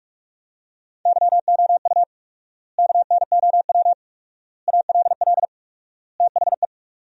Morse code, heard as a keyed tone, about 46 wpm, 700 Hz. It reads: YOU KNOW ALL THE
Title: 4x Word-Spacing - Sentences from Top 300 Words - 45wpm